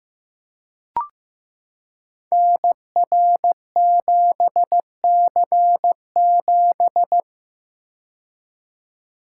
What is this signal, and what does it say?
Morse code, heard as a keyed tone, about 15 wpm, 700 Hz. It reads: NR7C7
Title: Code Groups - 5 Letter:Numbers - 15wpm